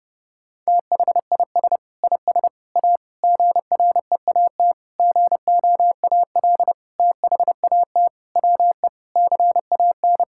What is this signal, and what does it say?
Morse code, heard as a keyed tone, about 32 wpm, 700 Hz. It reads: THIS IS A GREAT GOAL THAT WE CAN
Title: Sentences from Top 1000 Words - Review Part 6 - 30wpm